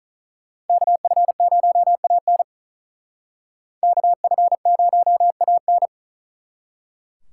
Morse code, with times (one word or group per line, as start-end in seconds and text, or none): 0.7-2.4 KF0AN
3.8-5.9 KF0AN